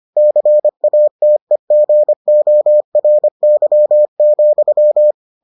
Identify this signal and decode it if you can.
Morse code, heard as a keyed tone, about 25 wpm, 600 Hz. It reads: CATEGORY,